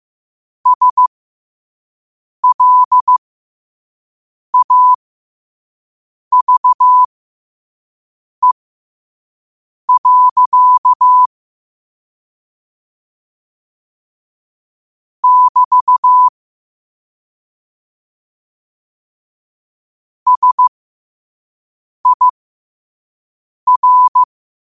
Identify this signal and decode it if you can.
Morse code, heard as a keyed tone, about 15 wpm, spaced as for 5 wpm, 1000 Hz. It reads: SLAVE. <BT> SIR
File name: qotdfiles / 5wpm